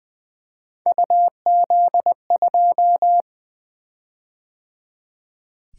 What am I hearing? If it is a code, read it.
Morse code, heard as a keyed tone, about 20 wpm, 700 Hz. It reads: UZ2
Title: Three Letter-Number Combinations - 20wpm